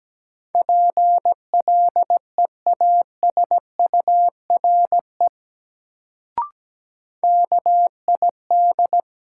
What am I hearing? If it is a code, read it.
Morse code, heard as a keyed tone, about 17 wpm, 700 Hz. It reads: PLEASURE KID
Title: Sentences from Top 1600 Words - New Word Drill - 17wpm